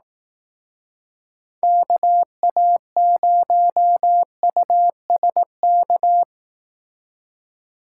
Morse code, heard as a keyed tone, about 18 wpm, 700 Hz. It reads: KA0USK
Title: Speed Racing - USA 2x3 Call Signs 15wpm